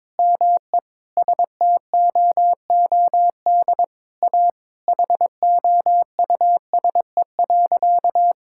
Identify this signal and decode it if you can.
Morse code, heard as a keyed tone, about 22 wpm, 700 Hz. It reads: ME STOOD A HOUSE.
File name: practice / 22wpm / Sentences with 4-Word Phrases - 22wpm